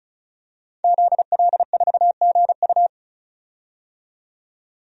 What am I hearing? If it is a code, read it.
Morse code, heard as a keyed tone, about 36 wpm, 700 Hz. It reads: ZL4GU
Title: DXCC-100 Call Signs - 5 Characters - 35wpm